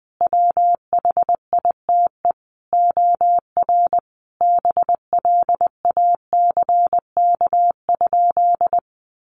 Morse code, heard as a keyed tone, about 20 wpm, 700 Hz. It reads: WHITE OR BLACK?
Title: Sentences from Top 400 Words 20wpm